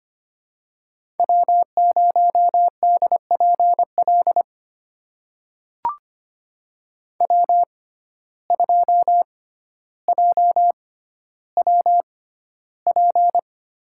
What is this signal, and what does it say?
Morse code, heard as a keyed tone, about 25 wpm, spaced as for 13 wpm, 700 Hz. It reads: W0DPL W2JWP
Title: USA Call Signs - 5 Characters - 6x Character Spacing - 25wpm